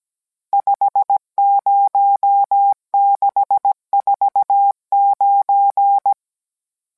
Morse code, heard as a keyed tone, about 17 wpm, 800 Hz. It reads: 50649